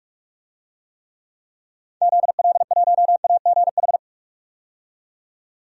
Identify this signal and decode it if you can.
Morse code, heard as a keyed tone, about 46 wpm, 700 Hz. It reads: ZL1AGH
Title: DXCC-100 Call Signs - 6 Characters - 45wpm